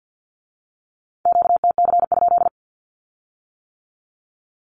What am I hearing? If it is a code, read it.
Morse code, heard as a keyed tone, about 50 wpm, 700 Hz. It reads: QTC?
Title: Q-Codes with Spoken Meaning - 50wpm